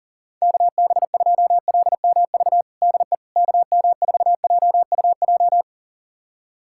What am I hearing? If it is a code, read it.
Morse code, heard as a keyed tone, about 37 wpm, 700 Hz. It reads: KB2LMV DE KM4JUJ
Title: USA Callsign DE Callsign - 6 Characters - 40wpm